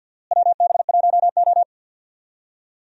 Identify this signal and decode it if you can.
Morse code, heard as a keyed tone, about 50 wpm, 700 Hz. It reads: WB1O
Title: USA Call Signs - 4 Characters - Repeated 3x - 50wpm